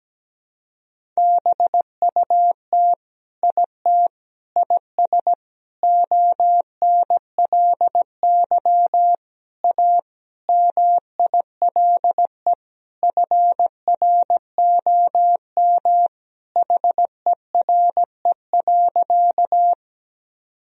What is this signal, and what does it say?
Morse code, heard as a keyed tone, about 17 wpm, 700 Hz. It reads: BUT IT IS ONLY A MILE FROM HERE.
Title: Sentences from Top 1000 Words - Review Part 1 - 17wpm